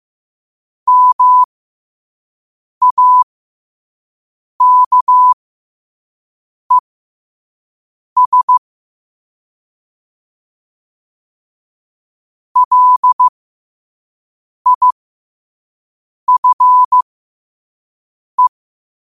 Morse code, heard as a keyed tone, about 15 wpm, spaced as for 5 wpm, 1000 Hz. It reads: MAKES LIFE
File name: qotdfiles / 5wpm